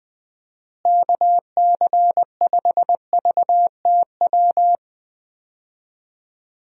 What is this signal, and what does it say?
Morse code, heard as a keyed tone, about 20 wpm, 700 Hz. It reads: KC5VTW
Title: USA Call Signs - 6 Characters - Repeated 2x - 20wpm